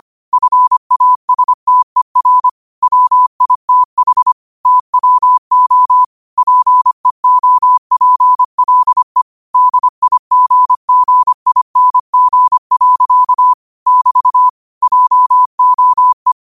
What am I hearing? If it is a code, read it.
Morse code, heard as a keyed tone, about 25 wpm, 1000 Hz. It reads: FASTER WITH TWO PEOPLE DIGGING. <BT> JOE
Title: 25 WPM morse code quotes for Thu, 29 Feb 2024